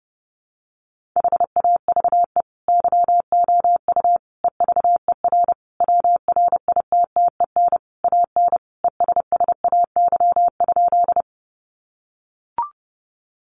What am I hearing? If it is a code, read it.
Morse code, heard as a keyed tone, about 30 wpm, 700 Hz. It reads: HAVE YOU EVER WRITTEN AN ESSAY?
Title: Sentences from Top 1500 Words 30wpm